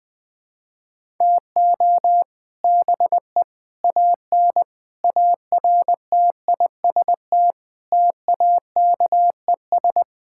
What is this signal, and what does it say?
Morse code, heard as a keyed tone, about 21 wpm, 700 Hz. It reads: TO BE AN ARTIST TAKES
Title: Sentences with 3-Word Phrases - 20wpm